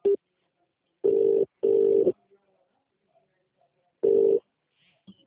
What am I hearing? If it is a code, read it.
Ringback tone (UK), 400 + 450 Hz.